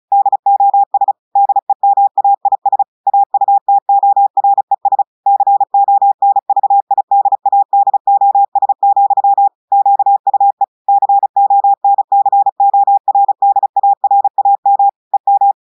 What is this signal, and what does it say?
Morse code, heard as a keyed tone, about 36 wpm, 800 Hz. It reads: DOS DEMAIS AUTORES CONVIDADOS, QUE CONCORDARAM EM
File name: audios / portugais / portugais texte en clair 1 vitesse 35 mots minutes